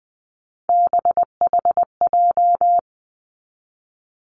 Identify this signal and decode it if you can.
Morse code, heard as a keyed tone, about 20 wpm, 700 Hz. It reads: BHJ
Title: Three Challenging Letter-Number Combinations 20wpm